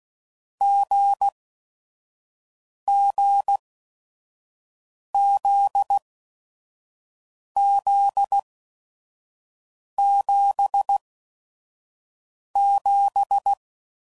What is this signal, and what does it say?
Morse code, heard as a keyed tone, about 16 wpm, spaced as for 5 wpm, 800 Hz. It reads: GGZZ77